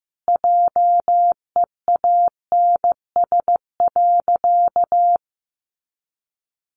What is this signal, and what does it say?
Morse code, heard as a keyed tone, about 15 wpm, 700 Hz. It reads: JEANS.
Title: Additional Easy Phrases 15wpm